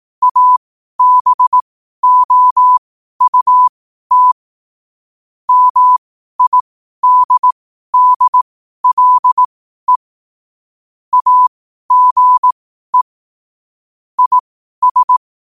Morse code, heard as a keyed tone, about 18 wpm, spaced as for 12 wpm, 1000 Hz. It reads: ABOUT MIDDLE AGE IS
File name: qotdfiles / 12wpm